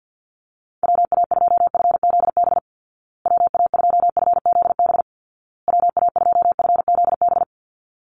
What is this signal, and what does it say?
Morse code, heard as a keyed tone, about 50 wpm, 700 Hz. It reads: WA2FZB WA2FZB WA2FZB
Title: Call Signs Repeated 2x - 50wpm